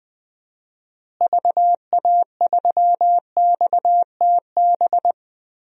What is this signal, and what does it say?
Morse code, heard as a keyed tone, about 21 wpm, 700 Hz. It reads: VA3XTB
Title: DXCC-100 Call Signs - 6 Characters - Repeated 3x - 20wpm